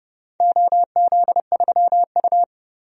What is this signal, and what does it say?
Morse code, heard as a keyed tone, about 30 wpm, 700 Hz. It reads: OZ3U
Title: DXCC-100 Call Signs - 4 Characters - 30wpm